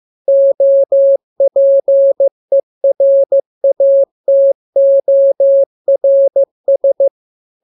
Morse code, heard as a keyed tone, about 15 wpm, 550 Hz. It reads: OPERATORS
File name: jokes / joke 2 15 0000